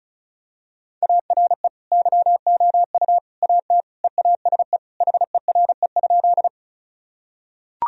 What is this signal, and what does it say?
Morse code, heard as a keyed tone, about 34 wpm, 700 Hz. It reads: ARE YOU AT EASE HERE?
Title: Sentences from Top 1000 Words - Review Part 1 - 35wpm